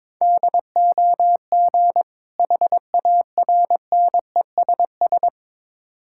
Morse code, heard as a keyed tone, about 22 wpm, 700 Hz. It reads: DOG HARNESS
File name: Difficult Words with Hints - 22wpm